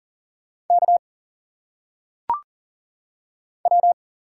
Morse code, heard as a keyed tone, about 40 wpm, 700 Hz. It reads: K W
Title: Single Letters - Phonetic Pronunciation 40wpm